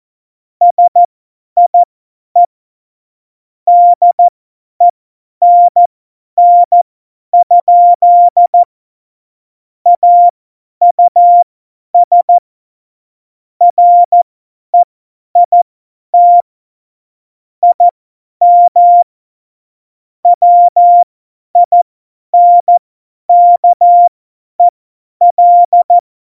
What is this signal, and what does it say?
Morse code, heard as a keyed tone, about 14 wpm, spaced as for 10 wpm, 700 Hz. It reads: SIE DENN? AUS REIT IM WINKEL
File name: Morsekurs / Witze / Witz 04 August 2020, 50 F70